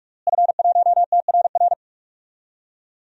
Morse code, heard as a keyed tone, about 46 wpm, 700 Hz. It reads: F1TRR
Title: DXCC-100 Call Signs - 5 Characters - Repeated 3x - 45wpm